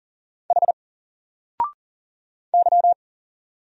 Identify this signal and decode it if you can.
Morse code, heard as a keyed tone, about 40 wpm, 700 Hz. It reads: H Y
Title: Single Letter-Number Alt 40wpm